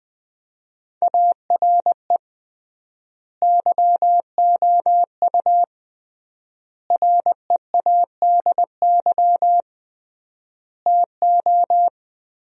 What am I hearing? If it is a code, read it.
Morse code, heard as a keyed tone, about 21 wpm, 700 Hz. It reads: ARE YOU READY TO
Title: 3x Word-Spacing - Sentences from Top 400 Words - 20wpm